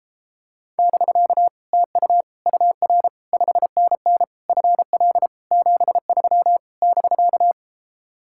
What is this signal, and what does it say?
Morse code, heard as a keyed tone, about 31 wpm, 700 Hz. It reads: <BK> TU UR 5NN FL 73 <BK>